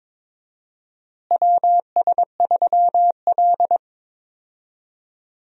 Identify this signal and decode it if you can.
Morse code, heard as a keyed tone, about 22 wpm, 700 Hz. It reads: WS3L
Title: USA Call Signs - 4 Characters - Repeated 2x - 22wpm